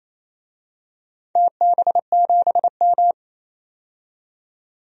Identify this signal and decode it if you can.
Morse code, heard as a keyed tone, about 27 wpm, 700 Hz. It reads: TB7M
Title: DXCC-100 Call Signs - 4 Characters - 28wpm